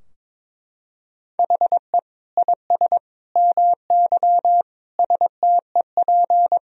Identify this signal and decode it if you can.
Morse code, heard as a keyed tone, about 22 wpm, 700 Hz. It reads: HE IS MY STEP
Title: Sentences from Top 1000 Words - Review Part 2 - 22wpm